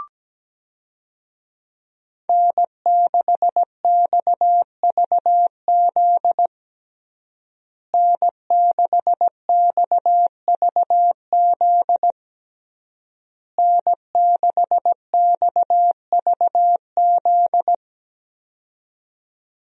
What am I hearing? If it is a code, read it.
Morse code, heard as a keyed tone, about 17 wpm, 700 Hz. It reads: N6XVZ N6XVZ N6XVZ
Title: USA Call Signs - 5 Characters - Repeated 3x - 17wpm